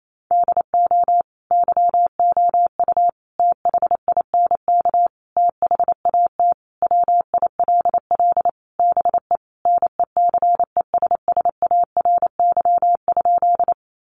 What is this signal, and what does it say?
Morse code, heard as a keyed tone, about 28 wpm, 700 Hz. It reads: DO YOU THINK THAT WILL BE NECESSARY?
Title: Sentences from Top 800 Words 28wpm